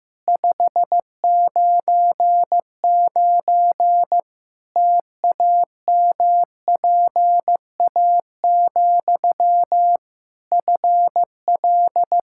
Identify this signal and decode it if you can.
Morse code, heard as a keyed tone, about 15 wpm, 700 Hz. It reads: 599 TAMPA, FL
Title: K1USN SST QSO Practice 15wpm